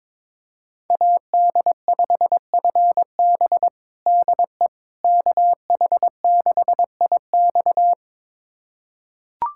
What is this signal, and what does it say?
Morse code, heard as a keyed tone, about 22 wpm, 700 Hz. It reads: AD5FB DE KH6IX
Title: USA Callsign DE Callsign - 5 Characters - 22wpm